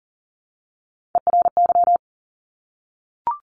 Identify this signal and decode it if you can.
Morse code, heard as a keyed tone, about 41 wpm, 700 Hz. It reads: ERY
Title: Top Three Letter Combinations in Words 40wpm